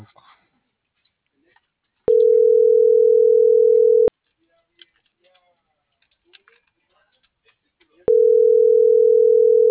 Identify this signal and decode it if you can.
Ringback tone (North America), 440 + 480 Hz.
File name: CENTRAL MABE / out antonio.garcia - .94783